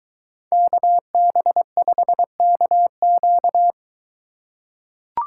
Morse code, heard as a keyed tone, about 22 wpm, 700 Hz. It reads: KB5KQ